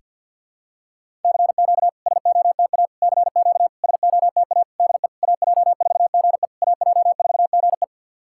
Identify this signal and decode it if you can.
Morse code, heard as a keyed tone, about 51 wpm, 690 Hz. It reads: CQ SOTA CQ SOTA DE AJ4GE AJ4GE
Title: SOTA QSO Practice 50wpm